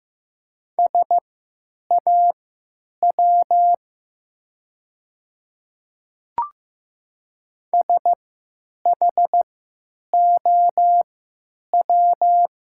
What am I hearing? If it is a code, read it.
Morse code, heard as a keyed tone, about 15 wpm, spaced as for 9 wpm, 700 Hz. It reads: SAW SHOW
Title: 3x Character Spacing - Top 300 Words - 15wpm